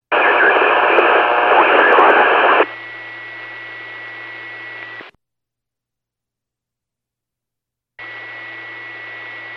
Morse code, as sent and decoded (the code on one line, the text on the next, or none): none